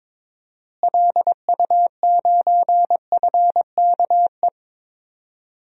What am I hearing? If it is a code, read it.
Morse code, heard as a keyed tone, about 22 wpm, 700 Hz. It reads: LU9FKE
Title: DXCC-100 Call Signs - 6 Characters - Repeated 3x - 22wpm